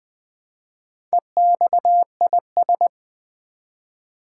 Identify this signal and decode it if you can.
Morse code, heard as a keyed tone, about 20 wpm, 700 Hz. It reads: EXIS
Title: Build A Word - 1 Letter At A Time - 5-Letter Words - 20wpm